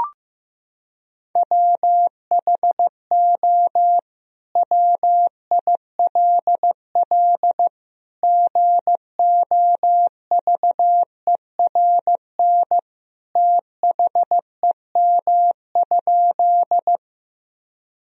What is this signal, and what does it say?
Morse code, heard as a keyed tone, about 15 wpm, 700 Hz. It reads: WHO WILL GOVERN THEM?
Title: Sentences from Top 1000 Words - Review Part 2 - 15wpm